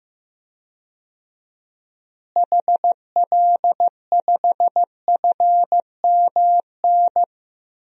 Morse code, heard as a keyed tone, about 15 wpm, 700 Hz. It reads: HL5FMN
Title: DXCC-100 Call Signs - 6 Characters - Repeated 2x - 15wpm